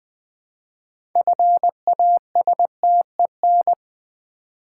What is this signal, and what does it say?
Morse code, heard as a keyed tone, about 20 wpm, 700 Hz. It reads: FASTEN
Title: Build A Word - 2 Letters At A Time - 6-Letter Words - 20wpm